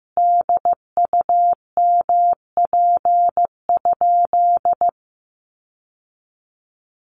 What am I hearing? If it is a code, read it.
Morse code, heard as a keyed tone, about 15 wpm, 700 Hz. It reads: DUMP?
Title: Sentences from Top 1300 Words 15wpm